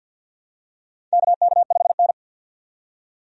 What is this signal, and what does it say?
Morse code, heard as a keyed tone, about 52 wpm, 700 Hz. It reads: KK5N
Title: USA Call Signs - 4 Characters - 50wpm